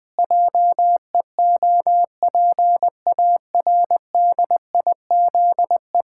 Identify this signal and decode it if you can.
Morse code, heard as a keyed tone, about 20 wpm, 700 Hz. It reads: JEOPARDIZE
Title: Pangram Challenge v2 - 20wpm